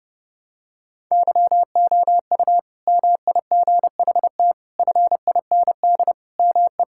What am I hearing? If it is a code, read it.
Morse code, heard as a keyed tone, about 30 wpm, 700 Hz. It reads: YOU MIGHT FIND ME
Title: Sentences from Top 1600 Words 30wpm